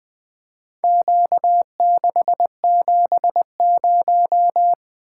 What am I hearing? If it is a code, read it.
Morse code, heard as a keyed tone, about 20 wpm, 700 Hz. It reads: Q670
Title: Four Letter-Number Combinations - 20wpm